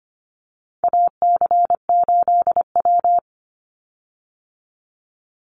Morse code, heard as a keyed tone, about 25 wpm, 700 Hz. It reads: AC8W
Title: Call Signs - 25wpm